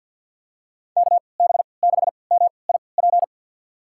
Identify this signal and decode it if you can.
Morse code, heard as a keyed tone, about 51 wpm, spaced as for 28 wpm, 700 Hz. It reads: KB6MIP
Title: USA Call Signs - 6 Characters - 3x Character Spacing - 50wpm